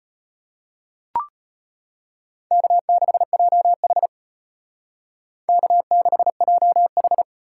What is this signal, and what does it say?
Morse code, heard as a keyed tone, about 30 wpm, 700 Hz. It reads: K6JH K6JH
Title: Speed Racing - USA 1x2 Call Signs 25wpm